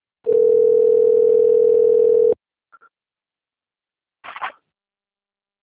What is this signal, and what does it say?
Ringback tone (North America), 440 + 480 Hz.
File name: rg-1600-unknown- - - .14147